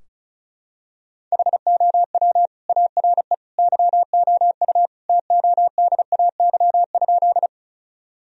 Morse code, heard as a keyed tone, about 34 wpm, 700 Hz. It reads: HOW ARE YOU TODAY?
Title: Sentences from Top 1000 Words - Review Part 4 - 35wpm